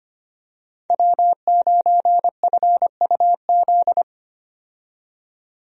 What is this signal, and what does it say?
Morse code, heard as a keyed tone, about 25 wpm, 690 Hz. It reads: W9FUZ
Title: USA Call Signs - 5 Characters - 25wpm